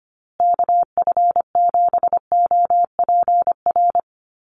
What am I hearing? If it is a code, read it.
Morse code, heard as a keyed tone, about 25 wpm, 700 Hz. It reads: KF7OPR
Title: USA Call Signs Repeated 2x 25wpm